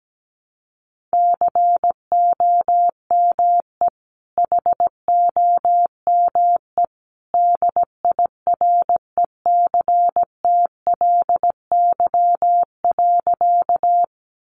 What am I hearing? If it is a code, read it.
Morse code, heard as a keyed tone, about 17 wpm, 700 Hz. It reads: COME HOME DIRECTLY.
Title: Sentences from Top 800 Words 17wpm